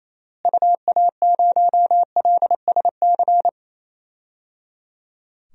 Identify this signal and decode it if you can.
Morse code, heard as a keyed tone, about 28 wpm, 700 Hz. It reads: UA0LSC
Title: DXCC-100 Call Signs - 6 Characters - Repeated 3x - 28wpm